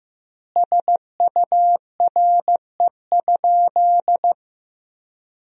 Morse code, heard as a keyed tone, about 15 wpm, 700 Hz. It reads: SURE?
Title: 2x Word-Spacing - Sentences from Top 300 Words - 15wpm